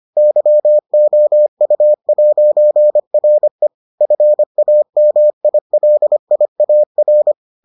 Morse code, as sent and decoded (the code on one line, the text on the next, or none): -.-- --- ..- .----. .-. / ..-. .- -- .. .-.. .. .- .-.
YOU'RE FAMILIAR